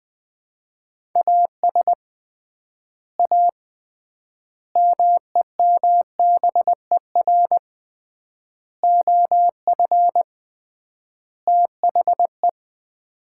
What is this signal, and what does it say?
Morse code, heard as a keyed tone, about 20 wpm, 700 Hz. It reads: AS A MEMBER OF THE
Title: 3x Word-Spacing - Sets of 5 Words - 20wpm